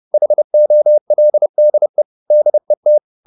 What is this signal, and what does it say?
Morse code, heard as a keyed tone, about 30 wpm, 600 Hz. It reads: HOLDE DET
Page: Ferie 30 wpm